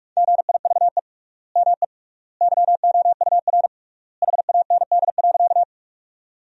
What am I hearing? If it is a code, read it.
Morse code, heard as a keyed tone, about 46 wpm, 700 Hz. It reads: GIVE ME YOUR HAND.
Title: 3x Word-Spacing - Sentences from Top 200 Words - 45wpm